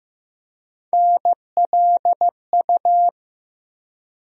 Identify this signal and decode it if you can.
Morse code, heard as a keyed tone, about 15 wpm, 700 Hz. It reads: NLU
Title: USA Call Sign Postfixes - 3 Letters 15wpm